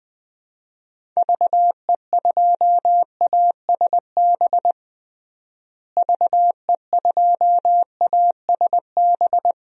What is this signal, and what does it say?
Morse code, heard as a keyed tone, about 21 wpm, 700 Hz. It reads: VE2ASB VE2ASB
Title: DXCC-100 Call Signs - 6 Characters - Repeated 3x - 20wpm